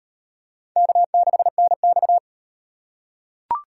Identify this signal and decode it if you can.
Morse code, heard as a keyed tone, about 33 wpm, 700 Hz. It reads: K6NX